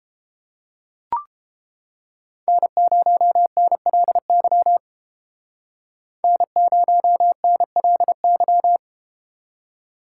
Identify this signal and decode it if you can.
Morse code, heard as a keyed tone, about 32 wpm, 700 Hz. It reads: N0NLY N0NLY